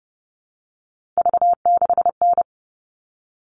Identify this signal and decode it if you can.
Morse code, heard as a keyed tone, about 30 wpm, 700 Hz. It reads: V6N
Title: All Cut Numbers - 3-Digit 30wpm